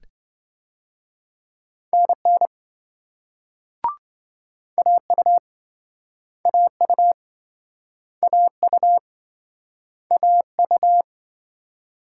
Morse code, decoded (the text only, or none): NN AU AU AU AU